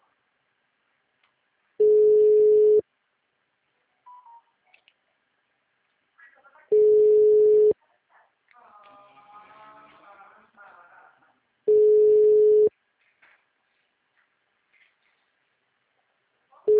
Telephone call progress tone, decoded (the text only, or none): ringback tone